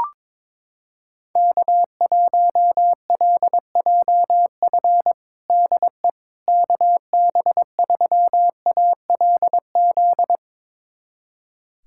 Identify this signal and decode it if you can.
Morse code, heard as a keyed tone, about 21 wpm, 700 Hz. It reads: K1LJF DE KB3ALZ
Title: USA Callsign DE Callsign 22wpm